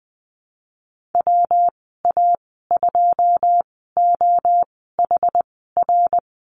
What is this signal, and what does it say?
Morse code, heard as a keyed tone, about 20 wpm, spaced as for 15 wpm, 700 Hz. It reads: WA2OHR
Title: 173 Lesson 28g - Number 9 - Call Signs 20wpm